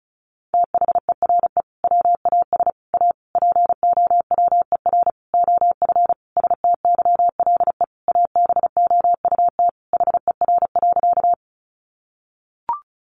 Morse code, heard as a keyed tone, about 34 wpm, 700 Hz. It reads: THERE WAS A POWER OF STYLE ABOUT HER.
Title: Sentences from Top 800 Words 35wpm